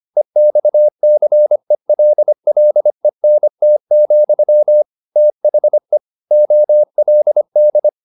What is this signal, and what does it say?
Morse code, heard as a keyed tone, about 25 wpm, 600 Hz. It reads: EXCELLENT, THE OLD
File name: the old man and the sea part 1 25wpm